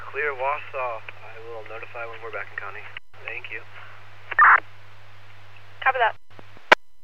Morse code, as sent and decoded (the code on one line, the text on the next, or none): none